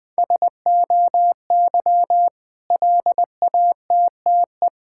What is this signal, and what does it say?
Morse code, heard as a keyed tone, about 20 wpm, 700 Hz. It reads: SOY LATTE
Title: Difficult Words with Hints - 20wpm